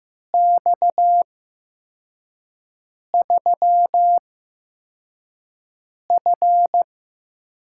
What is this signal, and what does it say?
Morse code, heard as a keyed tone, about 15 wpm, spaced as for 4 wpm, 700 Hz. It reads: X3F
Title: USA Call Signs - 8x Character Spacing - 15wpm